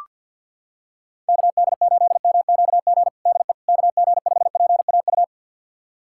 Morse code, heard as a keyed tone, about 46 wpm, 700 Hz. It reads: KD8MQG DE KG5PAU
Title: USA Callsign DE Callsign 50wpm